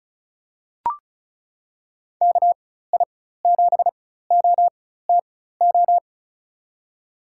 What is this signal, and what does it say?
Morse code, heard as a keyed tone, about 36 wpm, spaced as for 16 wpm, 700 Hz. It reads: KI7OTO
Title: USA Call Signs - 6 Characters - 4x Character Spacing - 35wpm